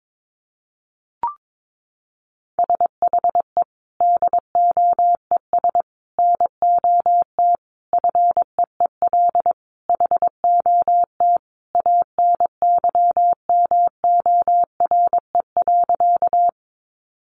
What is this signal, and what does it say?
Morse code, heard as a keyed tone, about 22 wpm, 700 Hz. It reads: SHE DOES NOT FEEL HOT ANYMORE.
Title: Easy Phrases 22wpm